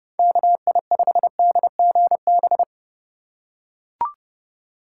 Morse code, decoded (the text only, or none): KI5DGB